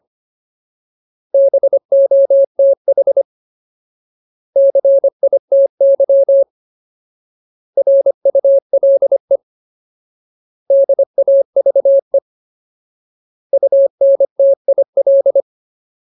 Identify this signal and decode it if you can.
Morse code, heard as a keyed tone, about 25 wpm, 550 Hz. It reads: BOTH CITY RULE DAVE UNTIL
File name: Ultimate Word List / 25WPM Ultimate